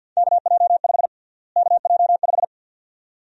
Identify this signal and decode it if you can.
Morse code, heard as a keyed tone, about 47 wpm, 700 Hz. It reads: KJ5 KJ5
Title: USA Call Sign Prefixes - Repeated 2x - 50wpm